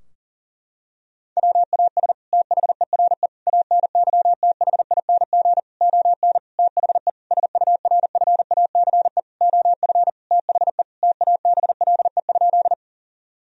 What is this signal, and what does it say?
Morse code, heard as a keyed tone, about 37 wpm, 700 Hz. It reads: WAS THERE ANYTHING ON THE SURFACE OF THE TABLE?
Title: Sentences from Top 1000 Words - Review Part 3 - 40wpm